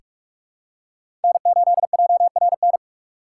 Morse code, heard as a keyed tone, about 48 wpm, 700 Hz. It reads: N8JRN
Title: USA Call Signs - 5 Characters - 4x Character Spacing - 45wpm